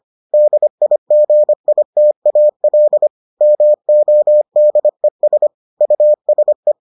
Morse code, heard as a keyed tone, about 25 wpm, 600 Hz. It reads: DIGITAL MODES USE